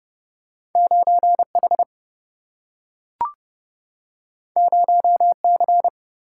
Morse code, heard as a keyed tone, about 30 wpm, 700 Hz. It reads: 9H 0C
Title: Single-Letter Single-Number 30wpm